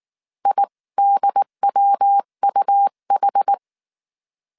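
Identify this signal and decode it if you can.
Morse code, heard as a keyed tone, about 19 wpm, 800 Hz. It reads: IDWUH
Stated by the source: Morse code dictation at 18 wpm from the IW3SGT QRS Morse code training course, five-character groups of 300 characters.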